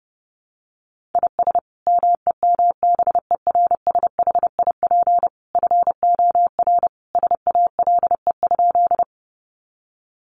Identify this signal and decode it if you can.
Morse code, heard as a keyed tone, about 30 wpm, 700 Hz. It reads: IS MEMBERSHIP FOR SALE?
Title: Sentences from Top 1400 Words 30wpm